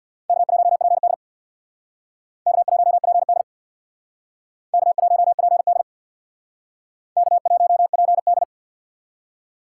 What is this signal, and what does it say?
Morse code, heard as a keyed tone, about 62 wpm, 700 Hz. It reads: K1PD K1PD K1PD K1PD
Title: Speed Racing - USA Call Signs 50wpm